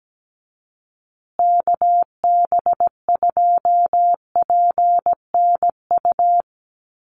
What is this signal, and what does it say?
Morse code, heard as a keyed tone, about 17 wpm, 700 Hz. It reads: KB2PNU
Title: USA Call Signs Repeated 2x 17wpm